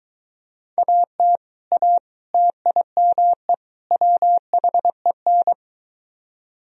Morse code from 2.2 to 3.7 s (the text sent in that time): TIME